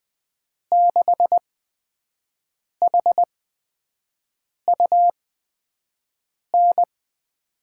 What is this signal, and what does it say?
Morse code, heard as a keyed tone, about 20 wpm, spaced as for 5 wpm, 700 Hz. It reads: 6HUN
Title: USA Call Signs - 8x Character Spacing - 20wpm